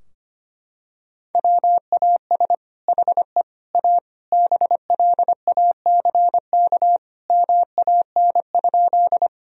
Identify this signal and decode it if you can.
Morse code, heard as a keyed tone, about 25 wpm, 700 Hz. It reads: WAS HE A BLACK MAN?
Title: Sentences from Top 1000 Words - Review Part 2 - 25wpm